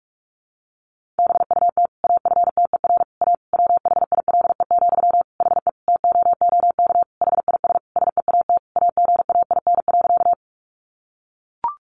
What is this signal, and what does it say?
Morse code, heard as a keyed tone, about 44 wpm, 700 Hz. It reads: BUT AFTER A WHILE, HE TOOK HIS SEAT AGAIN.
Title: Sentences from Top 1500 Words 45wpm